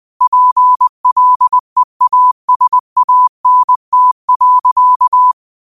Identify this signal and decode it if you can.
Morse code, heard as a keyed tone, about 20 wpm, 1000 Hz. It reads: PLEASANT.
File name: qotdfiles / 20wpm